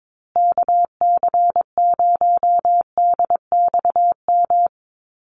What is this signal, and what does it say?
Morse code, heard as a keyed tone, about 22 wpm, 700 Hz. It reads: KC0DXM
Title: USA Call Signs Repeated 2x 22wpm